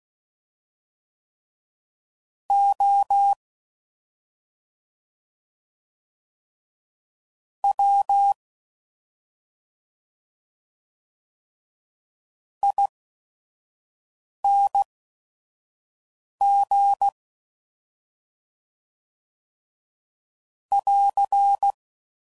Morse code, heard as a keyed tone, about 16 wpm, spaced as for 5 wpm, 800 Hz. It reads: O W ING <AR>